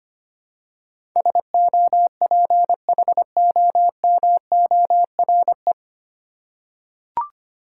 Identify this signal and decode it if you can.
Morse code, heard as a keyed tone, about 25 wpm, 700 Hz. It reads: SOPHOMORE
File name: Nine Letter Words - 25wpm